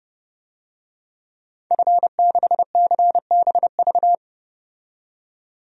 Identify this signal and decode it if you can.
Morse code, heard as a keyed tone, about 30 wpm, 700 Hz. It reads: F6CBV
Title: DXCC-100 Call Signs - 5 Characters - 30wpm